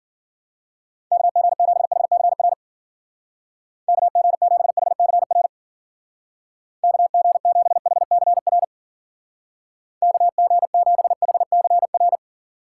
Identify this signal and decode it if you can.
Morse code, heard as a keyed tone, about 52 wpm, 700 Hz. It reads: KG7HCR KG7HCR KG7HCR KG7HCR